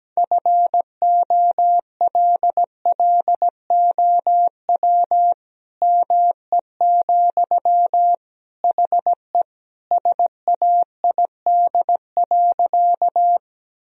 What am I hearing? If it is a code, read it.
Morse code, heard as a keyed tone, about 17 wpm, 700 Hz. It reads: FOLLOW ME, HE SAID.
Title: Sentences from Top 1000 Words - Review Part 1 - 17wpm